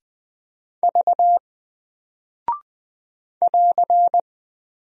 Morse code, heard as a keyed tone, about 20 wpm, 700 Hz. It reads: V <AR>
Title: Single Character (Letter:Number:Punctuation:Prosign) 20wpm